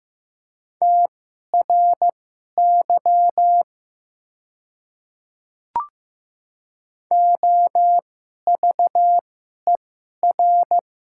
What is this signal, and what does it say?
Morse code, heard as a keyed tone, about 15 wpm, spaced as for 11 wpm, 700 Hz. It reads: TRY OVER
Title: 2x Character Spacing - Top 300 Words - 15wpm